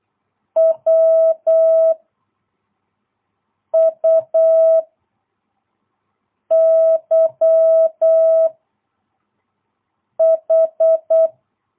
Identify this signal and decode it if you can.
Morse code, heard as a keyed tone, about 8 wpm, 650 Hz. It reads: W U Y H
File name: jh8cbh / 2022cw